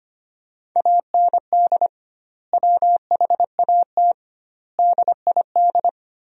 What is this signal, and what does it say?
Morse code, heard as a keyed tone, about 25 wpm, 700 Hz. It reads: AND WHAT DID
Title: 2x Word-Spacing - Sentences from Top 300 Words - 25wpm